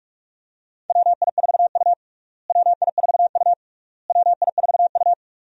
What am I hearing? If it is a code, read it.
Morse code, heard as a keyed tone, about 46 wpm, 700 Hz. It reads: WI4U WI4U WI4U
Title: USA Call Signs - 4 Characters - Repeated 3x - 45wpm